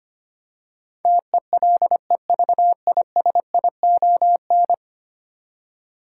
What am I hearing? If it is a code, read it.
Morse code, heard as a keyed tone, about 25 wpm, 700 Hz. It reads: TELEVISION
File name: Ten Letter Words - 25wpm